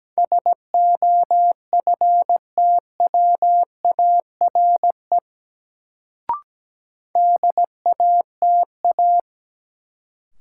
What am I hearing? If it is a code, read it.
Morse code, heard as a keyed tone, about 17 wpm, 700 Hz. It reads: SOFTWARE DATA
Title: Sentences from Top 1000 Words - Review Part 3 - New Word Drill - 17wpm